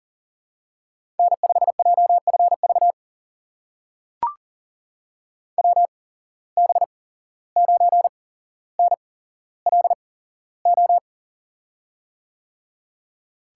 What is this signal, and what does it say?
Morse code, heard as a keyed tone, about 39 wpm, spaced as for 16 wpm, 700 Hz. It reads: N5JFV WB9NLO
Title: USA Call Signs - 8x Character Spacing - 40wpm